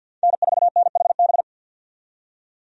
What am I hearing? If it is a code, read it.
Morse code, heard as a keyed tone, about 49 wpm, 700 Hz. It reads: N4NHB
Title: USA Call Signs - 5 Characters - 50wpm